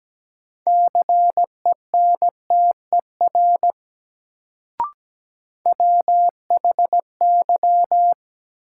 Morse code, heard as a keyed tone, about 17 wpm, 700 Hz. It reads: CENTER WHY
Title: Sets of 3 Words - New Word Drill - 17wpm